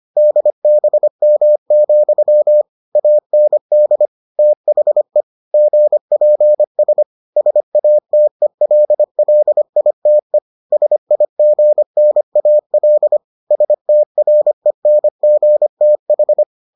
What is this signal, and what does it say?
Morse code, heard as a keyed tone, about 25 wpm, 600 Hz. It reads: DBM, AND THE GPS SATELLITE SIGNAL STRENGTH